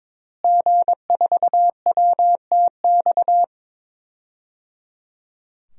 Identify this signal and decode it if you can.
Morse code, heard as a keyed tone, about 22 wpm, 700 Hz. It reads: G4WTX
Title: DXCC-100 Call Signs - 5 Characters - Repeated 3x - 22wpm